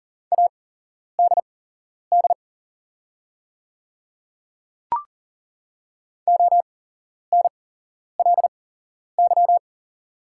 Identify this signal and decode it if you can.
Morse code, heard as a keyed tone, about 41 wpm, spaced as for 11 wpm, 690 Hz. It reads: ADD ONLY
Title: 8x Character Spacing - Top 300 Words - 40wpm